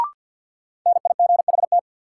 Morse code, heard as a keyed tone, about 53 wpm, 700 Hz. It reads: NIGHT
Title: Sentences from Top 300 Words - New Word Drill - 50wpm